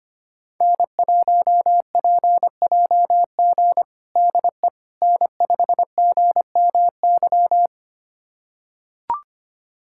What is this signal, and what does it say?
Morse code, heard as a keyed tone, about 25 wpm, 700 Hz. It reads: N1PJG DE N5GMY
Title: USA Callsign DE Callsign - 5 Characters - 25wpm